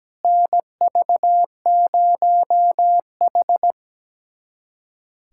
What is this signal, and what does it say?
Morse code, heard as a keyed tone, about 17 wpm, 700 Hz. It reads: NV0H
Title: USA Callsign DE Callsign - 4 Characters - 17wpm